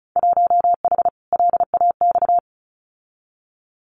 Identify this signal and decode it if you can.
Morse code, heard as a keyed tone, about 34 wpm, 700 Hz. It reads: 1H LAX
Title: Winter Field Day Exchanges 35wpm